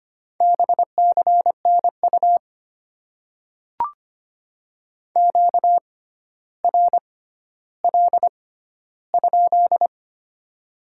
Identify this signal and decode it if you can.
Morse code, heard as a keyed tone, about 25 wpm, spaced as for 13 wpm, 700 Hz. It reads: BCNU QRL?
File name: Top 90 QSO Elements - 6x Character Spacing - 25wpm